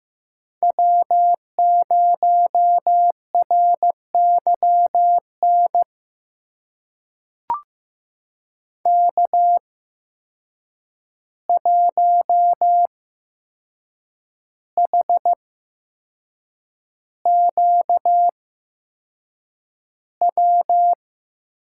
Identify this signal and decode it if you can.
Morse code, heard as a keyed tone, about 15 wpm, spaced as for 6 wpm, 700 Hz. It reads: W0RYN K1HQW
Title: USA Call Signs - 5 Characters - 8x Character Spacing - 15wpm